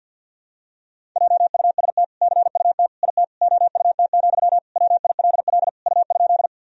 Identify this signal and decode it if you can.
Morse code, heard as a keyed tone, about 47 wpm, 700 Hz. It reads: JUST CUT IT OUT, WILL U?
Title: Sentences from Top 300 Words and Abbreviations 50wpm